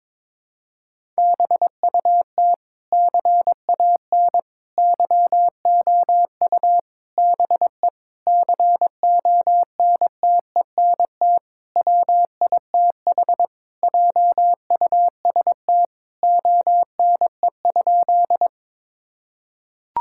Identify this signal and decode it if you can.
Morse code, heard as a keyed tone, about 22 wpm, 700 Hz. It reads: BUT CAN YOU BE CONTENT WITH JUST ONE?
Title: Sentences from Top 1000 Words - Review Part 6 - 22wpm